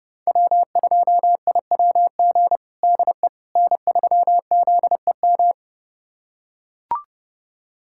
Morse code, heard as a keyed tone, about 30 wpm, 700 Hz. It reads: W2IWG DE N3ZEM
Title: USA Callsign DE Callsign - 5 Characters - 30wpm